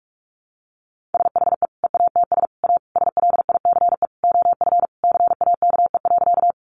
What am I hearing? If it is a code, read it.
Morse code, heard as a keyed tone, about 44 wpm, 700 Hz. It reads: SHE EATS A SLICE OF CAKE.
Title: Easy Phrases 45wpm